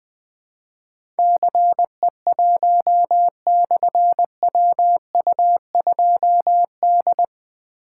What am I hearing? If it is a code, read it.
Morse code, heard as a keyed tone, about 20 wpm, 700 Hz. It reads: CE1/WU2D
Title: Difficult Call Signs 20wpm